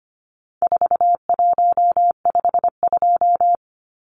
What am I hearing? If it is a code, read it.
Morse code, heard as a keyed tone, about 25 wpm, 700 Hz. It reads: 4152
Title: Numbers - 4-Digit 25wpm